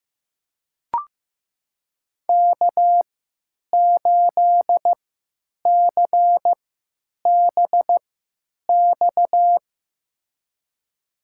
Morse code, heard as a keyed tone, about 15 wpm, spaced as for 9 wpm, 700 Hz. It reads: K8CBX
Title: USA Call Signs - 5 Characters - 3x Character Spacing - 15wpm